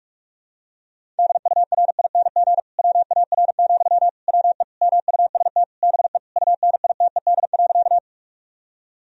Morse code, heard as a keyed tone, about 46 wpm, 700 Hz. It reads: DURING WAR, WE MUST BE UNITED.
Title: Sentences from Top 1000 Words - Review Part 4 - 45wpm